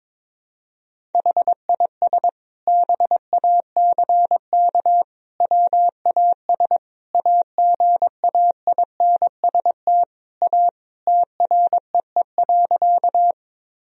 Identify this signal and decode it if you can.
Morse code, heard as a keyed tone, about 22 wpm, 700 Hz. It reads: HIS BACK WAS AGAINST A TREE.
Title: Sentences from Top 1000 Words - Review Part 1 - 22wpm